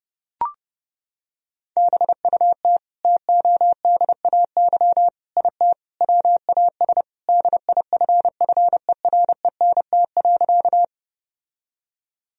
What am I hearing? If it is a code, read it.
Morse code, heard as a keyed tone, about 30 wpm, 700 Hz. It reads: BUT TODAY IT WAS DIFFERENT.
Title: Sentences from Top 1000 Words - Review Part 4 - 30wpm